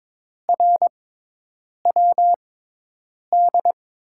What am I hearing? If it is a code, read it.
Morse code, heard as a keyed tone, about 21 wpm, spaced as for 8 wpm, 700 Hz. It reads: RWD
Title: USA Call Signs - 6 Characters - 6x Character Spacing - 22wpm